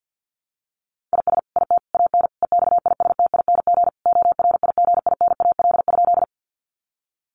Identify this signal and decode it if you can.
Morse code, heard as a keyed tone, about 50 wpm, 700 Hz. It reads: IS IT AN EXISTING ORIGINAL?
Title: Sentences from Top 1500 Words 50wpm